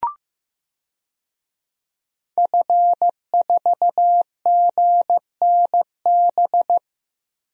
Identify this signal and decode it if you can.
Morse code, heard as a keyed tone, about 15 wpm, 700 Hz. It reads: F4GNB
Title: DXCC-100 Call Signs - 5 Characters - Repeated 2x - 15wpm